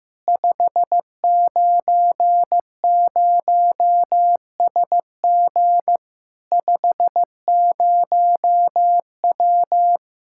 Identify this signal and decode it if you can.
Morse code, heard as a keyed tone, about 15 wpm, 700 Hz. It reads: 590SG 50W
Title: Practice QSO - Exchange 2 - Info - 15wpm